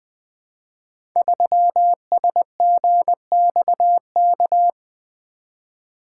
Morse code, heard as a keyed tone, about 20 wpm, 700 Hz. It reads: 3SGXK
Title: Code Groups - 5 Letter:Numbers - 20wpm